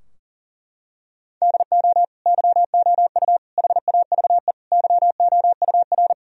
Morse code, heard as a keyed tone, about 40 wpm, 700 Hz. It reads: DO YOU HAVE YOUR
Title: Sentences from Top 1000 Words - Review Part 4 - 40wpm